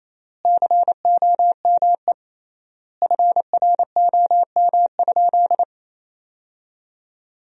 Morse code, heard as a keyed tone, about 27 wpm, 700 Hz. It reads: COME FROM?
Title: 3x Word-Spacing - Sentences from Top 200 Words - 28wpm